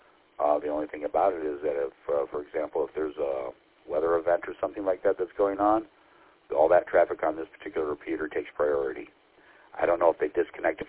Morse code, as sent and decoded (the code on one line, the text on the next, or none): none